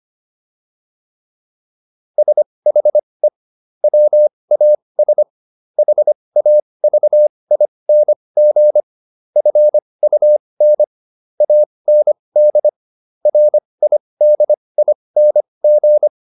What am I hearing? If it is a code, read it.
Morse code, heard as a keyed tone, about 25 wpm, spaced as for 20 wpm, 600 Hz. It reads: SHE WAS HAVING FUN AND RIDING
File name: SS123 20